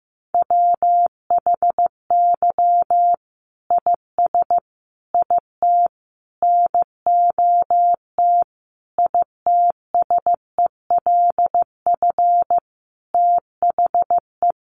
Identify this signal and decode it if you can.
Morse code, heard as a keyed tone, about 15 wpm, 700 Hz. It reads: WHY IS IT NOT ITSELF THE
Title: Sentences from Top 1400 Words 15wpm